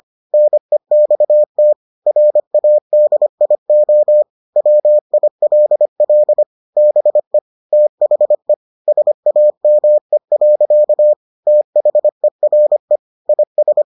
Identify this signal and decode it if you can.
Morse code, heard as a keyed tone, about 25 wpm, 600 Hz. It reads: NEXT RADIO WILL BE THE SAME. THERE IS